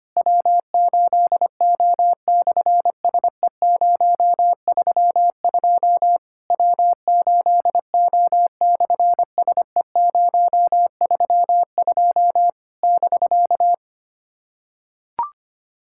Morse code, heard as a keyed tone, about 24 wpm, 700 Hz. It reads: W8O/SE032 W8O/SE032 <BK>
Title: SOTA QSO Practice 25wpm